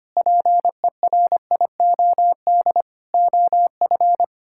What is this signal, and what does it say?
Morse code, heard as a keyed tone, about 25 wpm, 700 Hz. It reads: PERIOD OF
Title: Sentences from Top 1000 Words - Review Part 5 - 25wpm